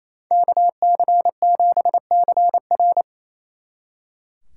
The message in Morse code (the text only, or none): KC7CR